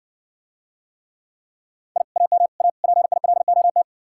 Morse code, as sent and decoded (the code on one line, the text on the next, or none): .. / .- -- / .- / .--. .. .-.. --- -
I AM A PILOT